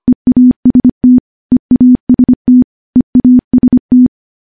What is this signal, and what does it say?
Morse code, heard as a keyed tone, about 25 wpm, 250 Hz. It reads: EAST EAST EAST
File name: morse / morse-training-25WPM